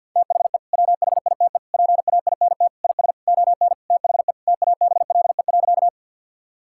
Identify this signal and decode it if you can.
Morse code, heard as a keyed tone, about 50 wpm, 700 Hz. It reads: THE WHITE PAINT IS ON THE TABLE.
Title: Sentences from Top 1000 Words - Review Part 6 - 50wpm